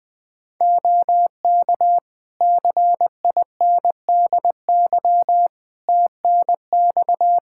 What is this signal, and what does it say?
Morse code, heard as a keyed tone, about 21 wpm, 700 Hz. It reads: OK CINDY TNX
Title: K1USN SST QSO Practice 20wpm